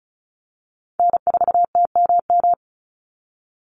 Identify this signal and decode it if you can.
Morse code, heard as a keyed tone, about 34 wpm, 700 Hz. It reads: N4TMM
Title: Call Signs Repeated 3x - 35wpm